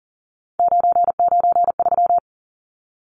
Morse code, heard as a keyed tone, about 40 wpm, 700 Hz. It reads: 993
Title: Numbers - 3-Digit 40wpm